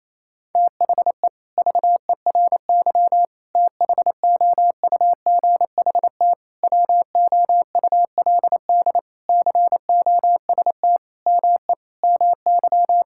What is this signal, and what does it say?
Morse code, heard as a keyed tone, about 27 wpm, 700 Hz. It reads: THE VERY THOUGHT WOULD COST ME MY
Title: Sentences from Top 1000 Words - Review Part 5 - 28wpm